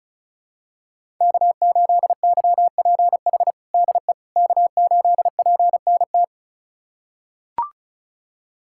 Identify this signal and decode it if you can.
Morse code, heard as a keyed tone, about 36 wpm, 690 Hz. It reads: K8YPH DE K8PNT
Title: USA Callsign DE Callsign - 5 Characters - 35wpm